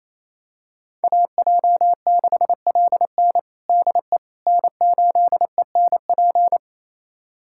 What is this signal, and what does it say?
Morse code, heard as a keyed tone, about 28 wpm, 700 Hz. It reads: AJ6LN DE N8ENP
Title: USA Callsign DE Callsign - 5 Characters - 28wpm